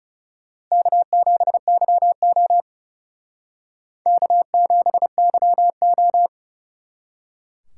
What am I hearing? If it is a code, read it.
Morse code, heard as a keyed tone, about 37 wpm, 700 Hz. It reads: K7YO K7YO
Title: Speed Racing - USA 1x2 Call Signs 30wpm